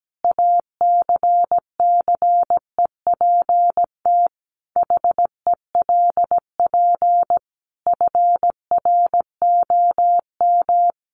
Morse code, heard as a keyed tone, about 17 wpm, 700 Hz. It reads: ACCEPT HELP FROM
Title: Sentences from Top 1300 Words 17wpm